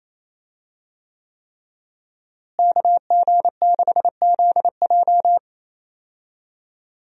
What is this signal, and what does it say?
Morse code, heard as a keyed tone, about 28 wpm, 700 Hz. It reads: KG6ZJ
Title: USA Call Signs - 5 Characters - Repeated 3x - 28wpm